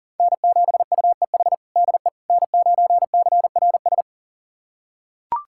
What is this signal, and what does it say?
Morse code, heard as a keyed tone, about 40 wpm, 700 Hz. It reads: N7UEH DE N9CRS
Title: USA Callsign DE Callsign - 5 Characters - 40wpm